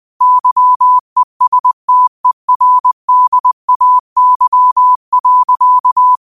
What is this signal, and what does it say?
Morse code, heard as a keyed tone, about 20 wpm, 1000 Hz. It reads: YESTERDAY.